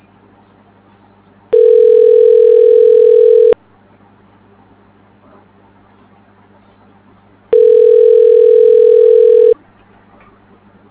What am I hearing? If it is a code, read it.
Ringback tone (North America), 440 + 480 Hz.